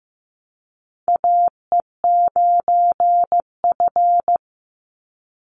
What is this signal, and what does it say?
Morse code, heard as a keyed tone, about 15 wpm, 700 Hz. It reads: AE9F
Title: NA SOTA Chaser Callsigns 15wpm